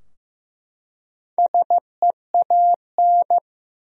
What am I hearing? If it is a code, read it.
Morse code, heard as a keyed tone, about 15 wpm, 700 Hz. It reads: SEAN
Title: USA Common Names - Max 5 Letters - 15wpm